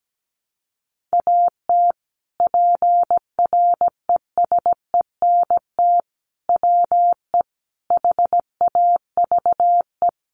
Morse code, heard as a keyed tone, about 17 wpm, 700 Hz. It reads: AT PRESENT WE HAVE
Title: Sentences from Top 800 Words 17wpm